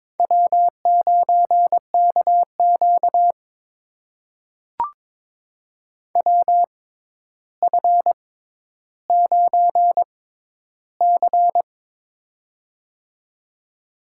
Morse code, heard as a keyed tone, about 22 wpm, spaced as for 11 wpm, 700 Hz. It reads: W9KQ WF9C
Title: USA Call Signs - 4 Characters - 6x Character Spacing - 22wpm